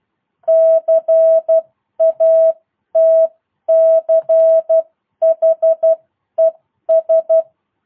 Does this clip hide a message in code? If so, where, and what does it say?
Yes: Morse code from 0.5 to 7.4 s, heard as a keyed tone, about 12 wpm, spaced as for 10 wpm, 650 Hz. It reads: CATCHES